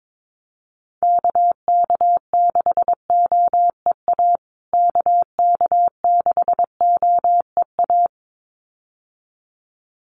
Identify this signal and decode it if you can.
Morse code, heard as a keyed tone, about 22 wpm, 700 Hz. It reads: KK6OEA KK6OEA
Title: USA Call Signs Repeated 2x 22wpm